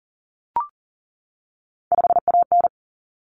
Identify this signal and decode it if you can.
Morse code, heard as a keyed tone, about 40 wpm, 700 Hz. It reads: 5AN
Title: Common Cut Numbers - 3-Digit 40wpm